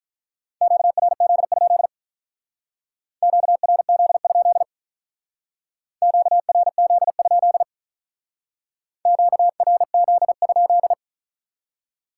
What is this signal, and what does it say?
Morse code, heard as a keyed tone, about 43 wpm, 700 Hz. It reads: QRZ? QRZ? QRZ? QRZ?
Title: Speed-Racing Common QSO Phrases - 35wpm